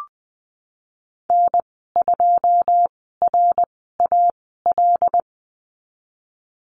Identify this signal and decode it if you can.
Morse code, heard as a keyed tone, about 20 wpm, spaced as for 15 wpm, 700 Hz. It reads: N2RAL
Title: 088 Lesson 15g - Number 5 - Call Signs 20wpm